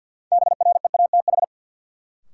Morse code, heard as a keyed tone, about 53 wpm, 700 Hz. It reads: BREATH
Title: Sets of 5 Words Encore - New Word Drill - 50wpm